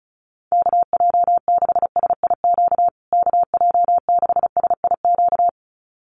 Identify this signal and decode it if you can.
Morse code, heard as a keyed tone, about 34 wpm, 700 Hz. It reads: KJ6SIQ KJ6SIQ
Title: USA Call Signs Repeated 2x 35wpm